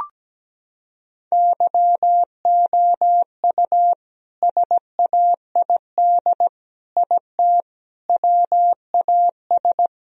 Morse code, heard as a keyed tone, about 17 wpm, 700 Hz. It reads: YOU SAID IT WAS
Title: Sentences from Top 1000 Words - Review Part 2 - 17wpm